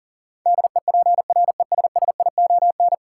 Morse code, heard as a keyed tone, about 40 wpm, 700 Hz. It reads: DEPRESSION
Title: Sentences from Top 1300 Words - New Word Drill - 40wpm